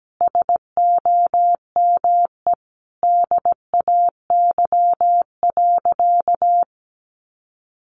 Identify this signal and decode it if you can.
Morse code, heard as a keyed tone, about 17 wpm, 700 Hz. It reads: SOME DAY.
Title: Sentences from Top 900 Words 17wpm